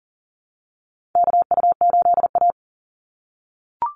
Morse code, heard as a keyed tone, about 40 wpm, 700 Hz. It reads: KU8A
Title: USA Call Signs 40wpm